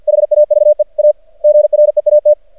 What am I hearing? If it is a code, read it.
Morse code, heard as a keyed tone, about 50 wpm, 600 Hz. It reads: HAVE A GREAT
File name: archived / CRSnet